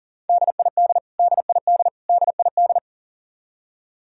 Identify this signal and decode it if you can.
Morse code, heard as a keyed tone, about 39 wpm, 700 Hz. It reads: DID DID DID
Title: Top 100 Words Repeated 40wpm